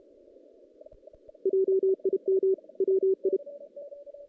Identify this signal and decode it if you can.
Morse code, heard as a keyed tone, about 31 wpm, 380 Hz. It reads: JIM WI
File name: 2016 JAN CW NAQP / 2016 Jan naqp k5ot - 0-6